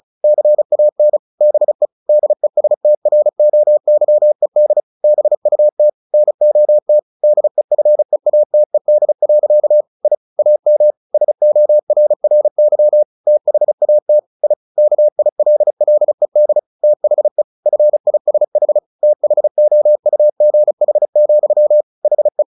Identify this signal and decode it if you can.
Morse code, heard as a keyed tone, about 34 wpm, 600 Hz. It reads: CAN BE DESTROYED BUT NOT DEFEATED. I AM SORRY THAT I KILLED THE FISH THOUGH, HE
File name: the old man and the sea 2 35wpm